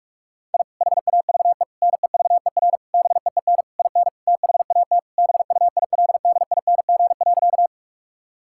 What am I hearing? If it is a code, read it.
Morse code, heard as a keyed tone, about 46 wpm, 700 Hz. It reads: I HAVE NEVER BEEN IN THAT BUILDING.
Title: Sentences from Top 1000 Words - Review Part 5 - 45wpm